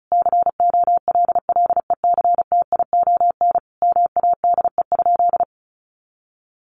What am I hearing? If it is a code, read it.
Morse code, heard as a keyed tone, about 34 wpm, 700 Hz. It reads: COLLECTION MADE?
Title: Sentences from Top 1200 Words 35wpm